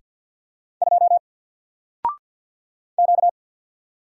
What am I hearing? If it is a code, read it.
Morse code, heard as a keyed tone, about 48 wpm, 700 Hz. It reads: 2 Q
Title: Single Letter-Number Alt 50wpm